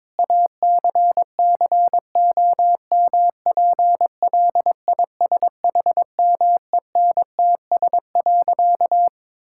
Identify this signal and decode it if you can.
Morse code, heard as a keyed tone, about 22 wpm, 700 Hz. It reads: ACCOMPLISHMENTS.
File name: Difficult Sentences with Hints - 22wpm